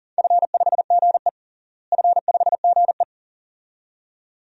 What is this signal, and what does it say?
Morse code, heard as a keyed tone, about 37 wpm, 700 Hz. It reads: F5GE F5GE
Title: DXCC-100 Call Signs - 4 Characters - Repeated 2x - 40wpm